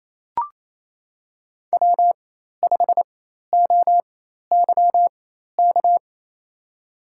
Morse code, heard as a keyed tone, about 28 wpm, spaced as for 13 wpm, 700 Hz. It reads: W5OYK
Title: USA Call Signs - 4x Character Spacing - 28wpm